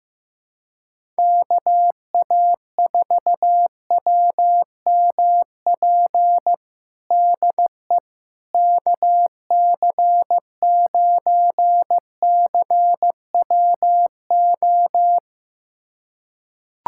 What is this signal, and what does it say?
Morse code, heard as a keyed tone, about 16 wpm, 700 Hz. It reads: KA4WMP DE KC9CWO
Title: USA Callsign DE Callsign - 6 Characters - 15wpm